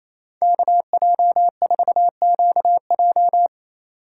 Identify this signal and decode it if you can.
Morse code, heard as a keyed tone, about 28 wpm, 700 Hz. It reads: KJ4QJ
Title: USA Call Signs - 5 Characters - 4x Character Spacing - 28wpm